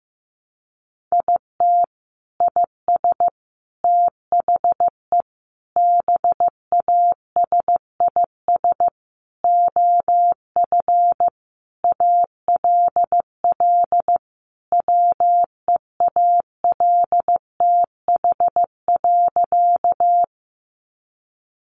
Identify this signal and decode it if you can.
Morse code, heard as a keyed tone, about 15 wpm, 700 Hz. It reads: IT IS THE BASIS OF ALL WEALTH.
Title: Sentences from Top 1200 Words 15wpm